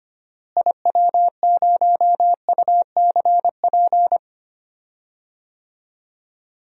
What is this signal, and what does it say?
Morse code, heard as a keyed tone, about 25 wpm, 700 Hz. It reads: IW0UCP
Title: DXCC-100 Call Signs - 6 Characters - Repeated 3x - 25wpm